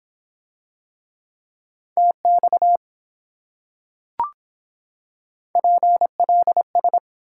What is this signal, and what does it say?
Morse code, heard as a keyed tone, about 25 wpm, 700 Hz. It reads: TX PLS